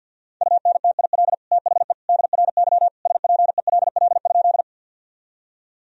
Morse code, heard as a keyed tone, about 52 wpm, 700 Hz. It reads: UNTIL THE DRY SPELL?
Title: Sentences from Top 1000 Words - Review Part 3 - 50wpm